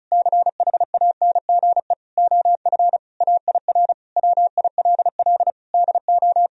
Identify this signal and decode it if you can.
Morse code, heard as a keyed tone, about 36 wpm, 700 Hz. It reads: CHANGE OF AIR WILL DO
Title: Sentences from Top 1000 Words - Review Part 1 - 35wpm